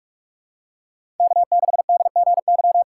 Morse code, heard as a keyed tone, about 46 wpm, 700 Hz. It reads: K6DGY
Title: USA Call Signs - 5 Characters - 4x Character Spacing - 45wpm